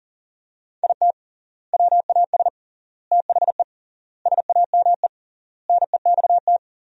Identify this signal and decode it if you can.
Morse code, heard as a keyed tone, about 40 wpm, 700 Hz. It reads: IT WAS THE SAME NEXT
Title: 3x Word-Spacing - Sentences from Top 500 Words - 40wpm